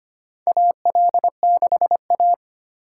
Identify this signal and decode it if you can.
Morse code, heard as a keyed tone, about 25 wpm, 700 Hz. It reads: AL6A
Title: USA Call Signs - 4 Characters - 6x Character Spacing - 25wpm